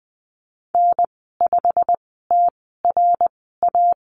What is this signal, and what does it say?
Morse code, heard as a keyed tone, about 20 wpm, spaced as for 15 wpm, 700 Hz. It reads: N5TRA
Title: 128 Lesson 21g - Number 6 - Call Signs 20wpm